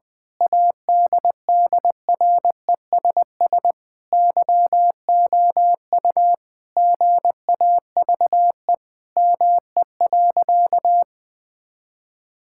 Morse code, heard as a keyed tone, about 20 wpm, 700 Hz. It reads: ADDRESS YOU GAVE ME.
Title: Sentences from Top 1000 Words - Review Part 6 - 20wpm